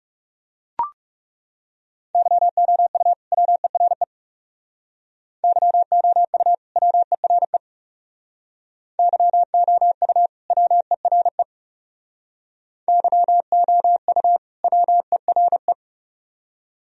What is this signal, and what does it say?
Morse code, heard as a keyed tone, about 39 wpm, 700 Hz. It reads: YOU WERE YOU WERE YOU WERE YOU WERE